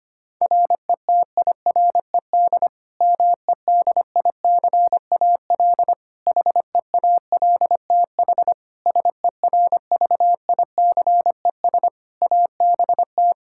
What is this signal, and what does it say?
Morse code, heard as a keyed tone, about 25 wpm, 700 Hz. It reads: RETIRED MEDICAL HEALTH SERVICES ABT
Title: K1USN SST QSO Practice 25wpm